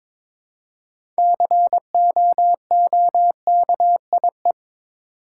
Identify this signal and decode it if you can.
Morse code, heard as a keyed tone, about 23 wpm, 700 Hz. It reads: COOKIE
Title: Six Letter Words - 22wpm